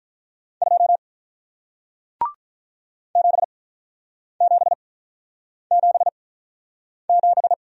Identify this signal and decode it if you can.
Morse code, heard as a keyed tone, about 48 wpm, spaced as for 9 wpm, 700 Hz. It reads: 2 7777